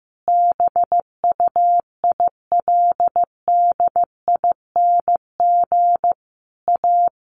Morse code, heard as a keyed tone, about 14 wpm, 700 Hz. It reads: BUILDING A
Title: Sentences from Top 700 Words 15wpm